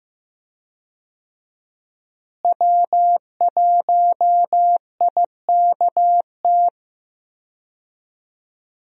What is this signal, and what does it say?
Morse code, heard as a keyed tone, about 15 wpm, 700 Hz. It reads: W1IKT
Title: USA Call Signs - 5 Characters - Repeated 2x - 15wpm